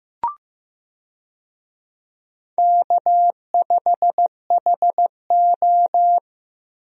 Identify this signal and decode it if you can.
Morse code, heard as a keyed tone, about 15 wpm, 700 Hz. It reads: K5HO
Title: USA Call Signs - 4 Characters - Repeated 3x - 15wpm